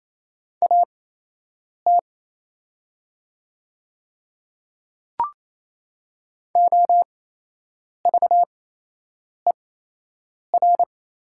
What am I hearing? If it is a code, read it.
Morse code, heard as a keyed tone, about 28 wpm, spaced as for 8 wpm, 700 Hz. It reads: AT OVER
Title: 8x Character Spacing - Top 100 Words - 28wpm